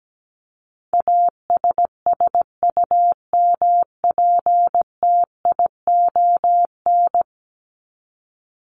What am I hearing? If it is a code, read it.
Morse code, heard as a keyed tone, about 17 wpm, 700 Hz. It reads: ASSUMPTION
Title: Top 1751-2000 Words 17wpm